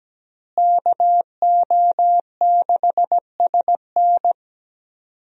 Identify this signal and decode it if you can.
Morse code, heard as a keyed tone, about 17 wpm, 700 Hz. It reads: KO6SN
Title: USA Call Signs - 5 Characters - Repeated 3x - 17wpm